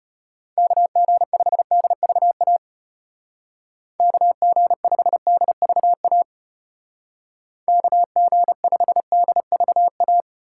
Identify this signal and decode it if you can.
Morse code, heard as a keyed tone, about 35 wpm, 700 Hz. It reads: KG5DVA KG5DVA KG5DVA